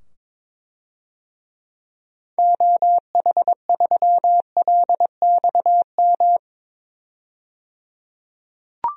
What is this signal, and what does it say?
Morse code, heard as a keyed tone, about 21 wpm, 700 Hz. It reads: OH3LXM
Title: DXCC-100 Call Signs - 6 Characters - 22wpm